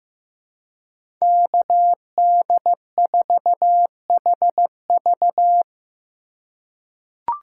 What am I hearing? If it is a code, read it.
Morse code, heard as a keyed tone, about 16 wpm, 700 Hz. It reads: KD4HV
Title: USA Call Signs - 5 Characters - 3x Character Spacing - 15wpm